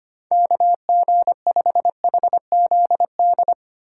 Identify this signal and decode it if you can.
Morse code, heard as a keyed tone, about 25 wpm, 700 Hz. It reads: KG5HZD
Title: USA Call Signs - 6 Characters - 2x Character Spacing - 25wpm